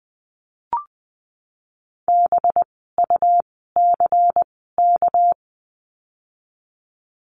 Morse code, heard as a keyed tone, about 20 wpm, spaced as for 14 wpm, 700 Hz. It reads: BUCK
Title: 199 Lesson 32f - Letter k - Words 20wpm